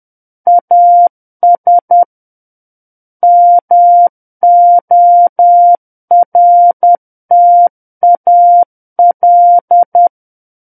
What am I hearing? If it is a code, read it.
Morse code, heard as a keyed tone, about 10 wpm, 700 Hz. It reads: AS MORTAL
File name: Book One / 01-Chapter 1 - 10 WPM